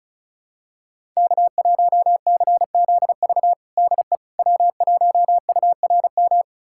Morse code, heard as a keyed tone, about 36 wpm, 700 Hz. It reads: K1CZV DE W1URM
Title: USA Callsign DE Callsign - 5 Characters - 35wpm